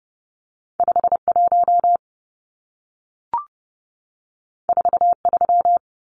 Morse code, heard as a keyed tone, about 30 wpm, 700 Hz. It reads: 51 43
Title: Numbers - 2-Digit 30wpm